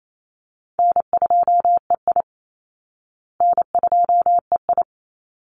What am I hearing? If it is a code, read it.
Morse code, heard as a keyed tone, about 28 wpm, 700 Hz. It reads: N2EI N2EI
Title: Call Signs Repeated 3x - 28wpm